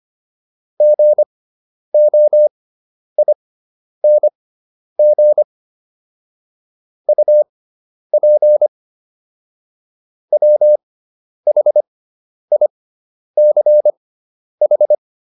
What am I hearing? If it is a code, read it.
Morse code, heard as a keyed tone, about 25 wpm, spaced as for 10 wpm, 600 Hz. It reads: GOING UP WHICH